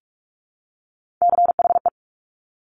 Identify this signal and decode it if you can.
Morse code, heard as a keyed tone, about 46 wpm, 700 Hz. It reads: CHE
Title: Top Three Letter Combinations in Words 45wpm